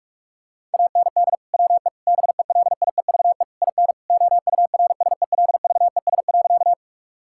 Morse code, heard as a keyed tone, about 46 wpm, 700 Hz. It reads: AND WE BELIEVE IN OURSELVES.
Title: Sentences from Top 1600 Words 45wpm